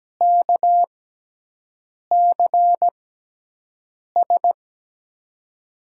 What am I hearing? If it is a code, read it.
Morse code, heard as a keyed tone, about 17 wpm, spaced as for 6 wpm, 700 Hz. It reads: KCS
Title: USA Call Signs - 5 Characters - 6x Character Spacing - 17wpm